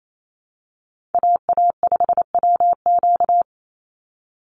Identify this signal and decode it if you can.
Morse code, heard as a keyed tone, about 28 wpm, 700 Hz. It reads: AA5WQ
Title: USA Call Signs 28wpm